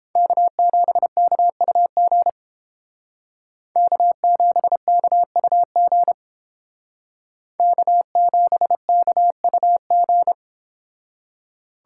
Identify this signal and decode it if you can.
Morse code, heard as a keyed tone, about 29 wpm, 700 Hz. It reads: K7KUG K7KUG K7KUG